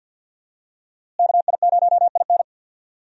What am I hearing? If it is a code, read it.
Morse code, heard as a keyed tone, about 50 wpm, 700 Hz. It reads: KI0IN
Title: USA Call Signs - 5 Characters - Repeated 2x - 50wpm